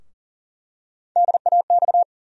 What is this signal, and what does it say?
Morse code, heard as a keyed tone, about 38 wpm, 700 Hz. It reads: DAX
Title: USA Call Sign Postfixes - 3 Letters 40wpm